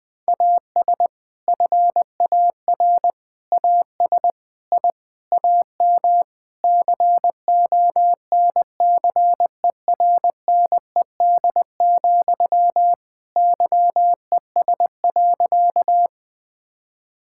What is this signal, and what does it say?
Morse code, heard as a keyed tone, about 20 wpm, 700 Hz. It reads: AS FAR AS I AM CONCERNED, YES.
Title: Sentences from Top 1000 Words - Review Part 6 - 20wpm